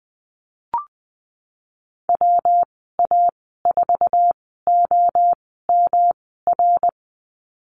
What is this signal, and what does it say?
Morse code, heard as a keyed tone, about 20 wpm, spaced as for 15 wpm, 700 Hz. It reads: WA4OMR
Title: 179 Lesson 29g - Slash - Call Signs 20wpm